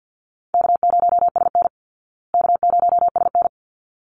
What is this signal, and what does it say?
Morse code, heard as a keyed tone, about 50 wpm, 700 Hz. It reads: K0SN K0SN
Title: Call Signs Repeated 3x - 50wpm